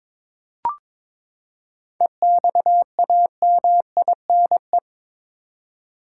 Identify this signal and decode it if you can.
Morse code, heard as a keyed tone, about 22 wpm, 700 Hz. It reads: EXAMINE
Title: Seven Letter Words - 22wpm